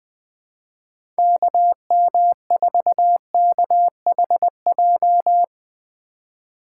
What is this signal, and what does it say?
Morse code, heard as a keyed tone, about 20 wpm, 700 Hz. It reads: KM4KHJ
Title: USA Call Signs - 6 Characters - 2x Character Spacing - 20wpm